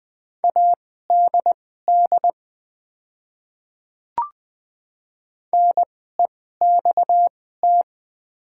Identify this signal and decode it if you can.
Morse code, heard as a keyed tone, about 20 wpm, spaced as for 14 wpm, 690 Hz. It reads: ADD NEXT
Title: 2x Character Spacing - Top 500 Words - 20wpm